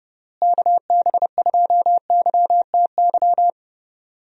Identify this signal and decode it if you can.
Morse code, heard as a keyed tone, about 30 wpm, 700 Hz. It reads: KB2YTY
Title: USA Call Signs - 6 Characters - Repeated 3x - 30wpm